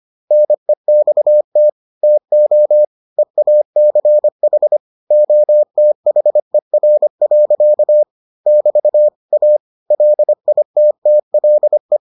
Morse code, heard as a keyed tone, about 25 wpm, 600 Hz. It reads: NEXT TO EACH OTHER. <BT> A LITTLE